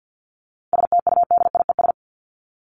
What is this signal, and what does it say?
Morse code, heard as a keyed tone, about 50 wpm, 700 Hz. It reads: STUDIES
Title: ES:S Suffix - Top 100 Words with Base Words 50wpm